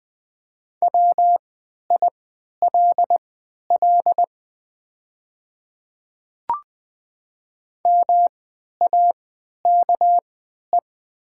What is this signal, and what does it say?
Morse code, heard as a keyed tone, about 20 wpm, spaced as for 11 wpm, 700 Hz. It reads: WILL MAKE
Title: 3x Character Spacing - Top 100 Words - 20wpm